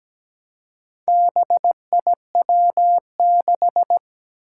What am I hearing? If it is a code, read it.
Morse code, heard as a keyed tone, about 17 wpm, 700 Hz. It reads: BIW6
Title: Four Challenging Letter-Number Combinations 17wpm